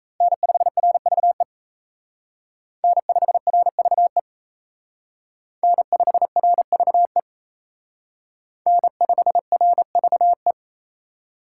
Morse code, heard as a keyed tone, about 35 wpm, 700 Hz. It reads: N5RVE N5RVE N5RVE N5RVE
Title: Speed Racing - USA 1x3 Call Signs 28wpm